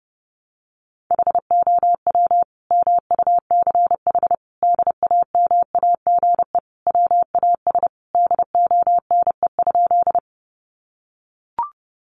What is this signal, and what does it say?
Morse code, heard as a keyed tone, about 30 wpm, 700 Hz. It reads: HOW MUCH DAMAGE WAS DONE?
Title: Sentences from Top 1100 Words 30wpm